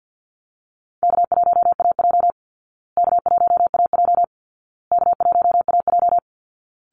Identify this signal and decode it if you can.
Morse code, heard as a keyed tone, about 48 wpm, 700 Hz. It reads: K1AJ K1AJ K1AJ
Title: Easy Call Signs Repeated 3x - 50wpm